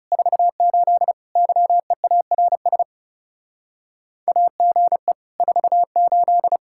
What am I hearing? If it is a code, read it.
Morse code, heard as a keyed tone, about 36 wpm, 700 Hz. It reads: 48 YEARS AGE 48
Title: Speed-Racing Common QSO Phrases - 30wpm